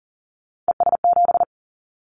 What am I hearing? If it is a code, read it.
Morse code, heard as a keyed tone, about 42 wpm, 700 Hz. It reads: ES7
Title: Callsign Prefixes for Top 50 DXCC - 40wpm